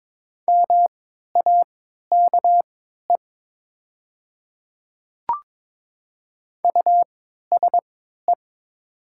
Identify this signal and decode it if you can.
Morse code, heard as a keyed tone, about 22 wpm, spaced as for 12 wpm, 700 Hz. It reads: MAKE USE
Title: 3x Character Spacing - Top 200 Words - 22wpm